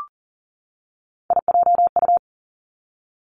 Morse code, heard as a keyed tone, about 42 wpm, 700 Hz. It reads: IJU
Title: Three Challenging Letter-Number Combinations 40wpm